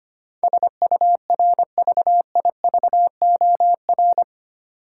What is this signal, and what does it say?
Morse code, heard as a keyed tone, about 25 wpm, 700 Hz. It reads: SURVIVOR
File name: Eight Letter Words - 25wpm